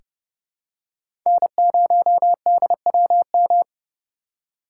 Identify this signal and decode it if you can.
Morse code, heard as a keyed tone, about 30 wpm, 700 Hz. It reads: N0DWM
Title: USA Call Signs - 5 Characters - 3x Character Spacing - 30wpm